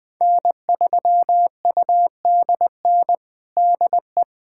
Morse code, heard as a keyed tone, about 20 wpm, 700 Hz. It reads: N3UDN DE
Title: USA Callsign DE Callsign - 5 Characters - 20wpm